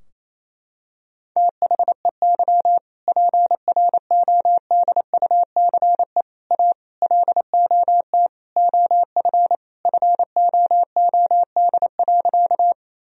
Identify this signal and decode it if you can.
Morse code, heard as a keyed tone, about 28 wpm, 700 Hz. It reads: THEY PRODUCE A LOT OF FOOD.
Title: Sentences from Top 1000 Words - Review Part 3 - 28wpm